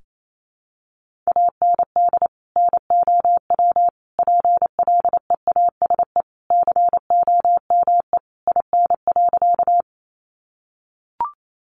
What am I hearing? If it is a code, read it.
Morse code, heard as a keyed tone, about 28 wpm, 700 Hz. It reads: AND NOW PLEASE COME IN.
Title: Sentences from Top 1500 Words 28wpm